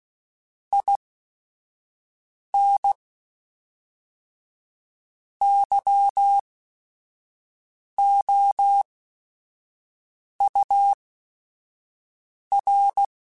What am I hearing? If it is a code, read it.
Morse code, heard as a keyed tone, about 16 wpm, spaced as for 5 wpm, 800 Hz. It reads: IN YOUR
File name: Quotes / Quotes-mp3s / ChildAsleep @5